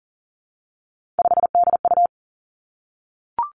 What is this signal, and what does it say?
Morse code, heard as a keyed tone, about 39 wpm, 700 Hz. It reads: 5DU
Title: Three Challenging Letter-Number Combinations 40wpm